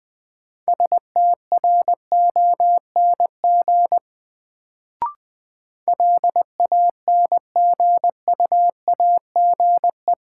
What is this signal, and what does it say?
Morse code, heard as a keyed tone, about 21 wpm, 700 Hz. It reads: STRONG LANGUAGE
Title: Sentences from Top 500 Words - New Word Drill - 20wpm